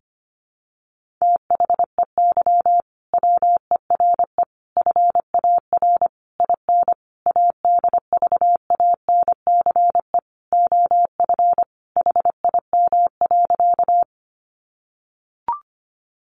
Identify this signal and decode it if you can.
Morse code, heard as a keyed tone, about 25 wpm, 700 Hz. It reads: THEY WERE FAR IN ADVANCE OF HIM.
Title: Sentences from Top 1400 Words 25wpm